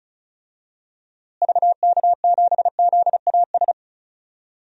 Morse code, heard as a keyed tone, about 36 wpm, 700 Hz. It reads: VK7ZAS
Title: DXCC-100 Call Signs - 6 Characters - Repeated 2x - 35wpm